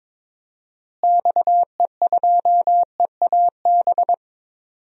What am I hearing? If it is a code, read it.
Morse code, heard as a keyed tone, about 21 wpm, 700 Hz. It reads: XE2EAB
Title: DXCC-100 Call Signs - 6 Characters - Repeated 3x - 22wpm